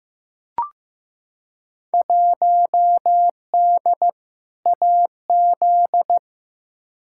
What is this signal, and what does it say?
Morse code, heard as a keyed tone, about 15 wpm, 700 Hz. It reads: 1D AZ
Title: Field Day Exchanges - 15wpm